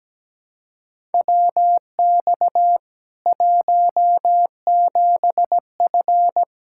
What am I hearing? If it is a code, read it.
Morse code, heard as a keyed tone, about 17 wpm, 700 Hz. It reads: WX 17F
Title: Practice QSO - Exchange 2 - Info - 17wpm